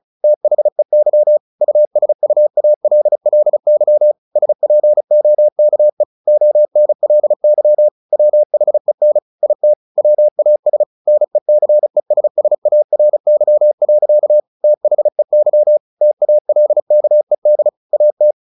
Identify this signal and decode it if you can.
Morse code, heard as a keyed tone, about 34 wpm, 600 Hz. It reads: THEY USUALLY SPOKE ONLY WHEN IT WAS NECESSARY. THEY TALKED AT
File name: the old man and the sea / the old man and the sea 1 35wpm